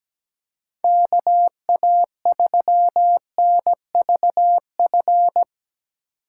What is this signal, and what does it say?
Morse code, heard as a keyed tone, about 17 wpm, 700 Hz. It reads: KA3NVF
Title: USA Call Signs - 6x Character Spacing - 17wpm